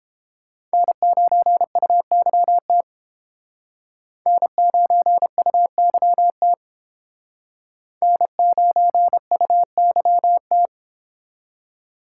Morse code, heard as a keyed tone, about 30 wpm, 700 Hz. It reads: N9UYT N9UYT N9UYT